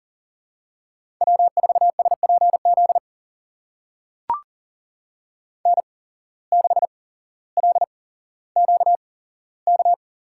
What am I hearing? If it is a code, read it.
Morse code, heard as a keyed tone, about 40 wpm, spaced as for 17 wpm, 700 Hz. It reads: W4SPZ N6LQK
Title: USA Call Signs - 5 Characters - 8x Character Spacing - 40wpm